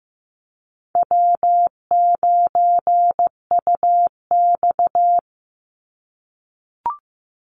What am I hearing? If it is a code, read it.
Morse code, heard as a keyed tone, about 15 wpm, 700 Hz. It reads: W9UX
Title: NA SOTA Chaser Callsigns 15wpm